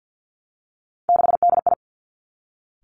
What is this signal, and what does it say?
Morse code, heard as a keyed tone, about 50 wpm, 700 Hz. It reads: 6DI
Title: Three Challenging Letter-Number Combinations 50wpm